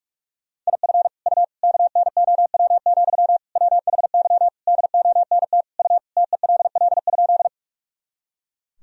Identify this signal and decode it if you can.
Morse code, heard as a keyed tone, about 41 wpm, 700 Hz. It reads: IF U KNOW, WHY DONT U TELL?
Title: Sentences from Top 300 Words and Abbreviations 45wpm